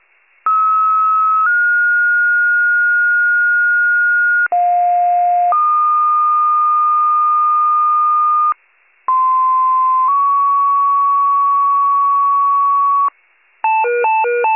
Two-tone page, heard as a one-second tone, then a three-second tone, about 1290 then 1430 Hz, about 710 then 1220 Hz, about 1040 then 1120 Hz. It is heard from 0.5 to 13.1 s.